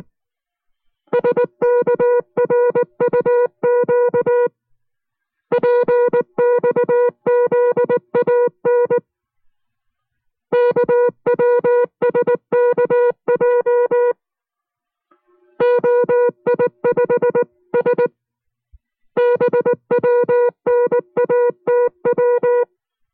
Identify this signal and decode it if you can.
Morse code, heard as a keyed tone, about 19 wpm, 490 Hz. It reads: SKRUQ PXZAN KWSKJ OI5SB WNATW